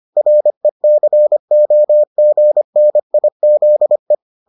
Morse code, heard as a keyed tone, about 25 wpm, 600 Hz. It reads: RECOGNIZE